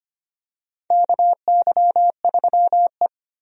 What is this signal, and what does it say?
Morse code, heard as a keyed tone, about 25 wpm, 700 Hz. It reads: KY3E
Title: USA Call Signs - 4 Characters - Repeated 2x - 25wpm